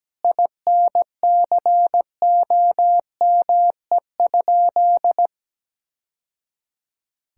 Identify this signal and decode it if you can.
Morse code, heard as a keyed tone, about 17 wpm, 700 Hz. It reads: INCOME?
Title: Sentences from Top 1000 Words - Review Part 5 - 17wpm